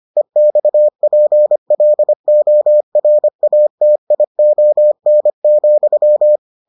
Morse code, heard as a keyed tone, about 25 wpm, 600 Hz. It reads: EXPLORATION,